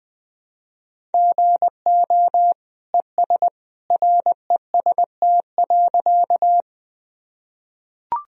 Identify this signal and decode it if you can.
Morse code, heard as a keyed tone, about 20 wpm, 700 Hz. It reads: GO ES REST.
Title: Sentences from Top 500 Words and Abbreviations 20wpm